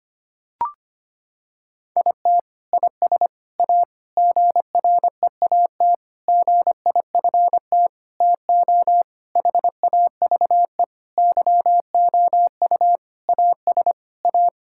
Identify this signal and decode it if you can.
Morse code, heard as a keyed tone, about 25 wpm, 700 Hz. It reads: IT IS A GREAT GIFT TO HAVE YOU AS A
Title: Sentences from Top 1000 Words - Review Part 6 - 25wpm